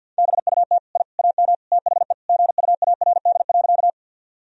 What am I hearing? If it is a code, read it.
Morse code, heard as a keyed tone, about 51 wpm, 700 Hz. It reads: BUT I AM THE GUARD.
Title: Sentences from Top 1000 Words - Review Part 4 - 50wpm